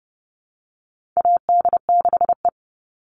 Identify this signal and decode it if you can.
Morse code, heard as a keyed tone, about 30 wpm, 700 Hz. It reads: AD6E
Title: Call Signs - 30wpm